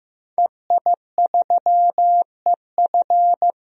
Morse code, heard as a keyed tone, about 15 wpm, 700 Hz. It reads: EI3EF
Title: DXCC-100 Call Signs - 5 Characters - 15wpm